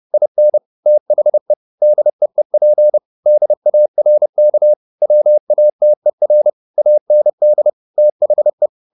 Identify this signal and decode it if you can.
Morse code, heard as a keyed tone, about 30 wpm, 600 Hz. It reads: IN THE DEEP DARK WATER AND THE
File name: the old man and the sea / the old man and the sea 1 30wpm